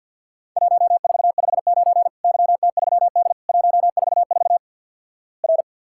Morse code, heard as a keyed tone, about 53 wpm, 700 Hz. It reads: 1459 YT3D 144 R
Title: Worked All Europe DX Contest - QTC Exchange - 50wpm